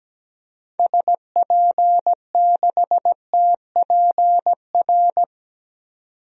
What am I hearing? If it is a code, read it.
Morse code, heard as a keyed tone, about 17 wpm, 700 Hz. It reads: SP6TPR
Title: DXCC-100 Call Signs - 6 Characters - 17wpm